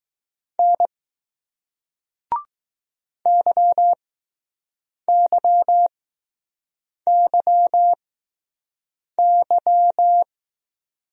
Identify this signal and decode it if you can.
Morse code, heard as a keyed tone, about 20 wpm, spaced as for 6 wpm, 700 Hz. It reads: N YYYY